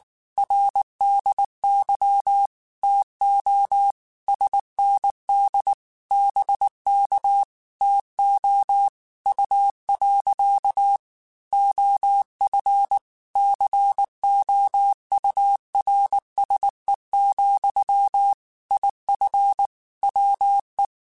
Morse code, heard as a keyed tone, about 19 wpm, 800 Hz. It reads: RDY TO SND BK TO U. OF COURSE, IF WE
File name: QSOs / QSOs-mp3s / WA7QOX 1@18